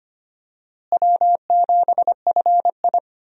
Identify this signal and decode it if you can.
Morse code, heard as a keyed tone, about 25 wpm, 700 Hz. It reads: W7FI
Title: USA Call Signs - 4 Characters - Repeated 3x - 25wpm